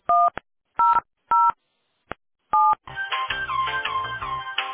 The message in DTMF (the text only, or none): 1007